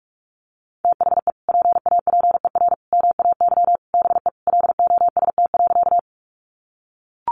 Morse code, heard as a keyed tone, about 44 wpm, 700 Hz. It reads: THE PAPER MAY BE LOST.
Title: Sentences from Top 1400 Words 45wpm